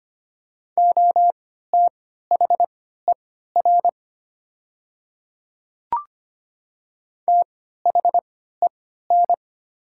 Morse code, heard as a keyed tone, about 25 wpm, spaced as for 14 wpm, 700 Hz. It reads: OTHER THEN
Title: 3x Character Spacing - Top 200 Words - 25wpm